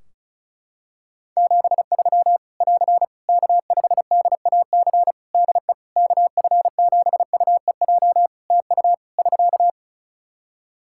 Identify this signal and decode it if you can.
Morse code, heard as a keyed tone, about 36 wpm, 700 Hz. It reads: 73 <AR> K5DAC DE KF7UEJ TU <SK>
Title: Practice QSO - Exchange 4 - Ending - 35wpm